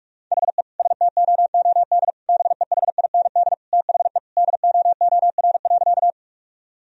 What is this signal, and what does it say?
Morse code, heard as a keyed tone, about 47 wpm, 700 Hz. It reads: HE STOOD BEHIND THE DOOR.
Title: Sentences from Top 1000 Words - Review Part 2 - 45wpm